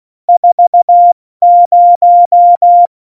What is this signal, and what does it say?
Morse code, heard as a keyed tone, about 16 wpm, 700 Hz. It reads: 40
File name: Morsekurs / Witze / Witz 15. Oktober 2019, 70 F80